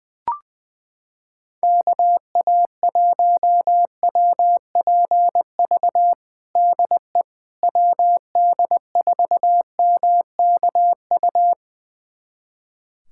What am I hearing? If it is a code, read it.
Morse code, heard as a keyed tone, about 20 wpm, 700 Hz. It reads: KA1WPV DE WD4MKU
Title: USA Callsign DE Callsign - 6 Characters - 20wpm